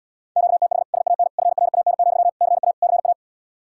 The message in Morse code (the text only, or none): QTH SAN FRAN, CA CA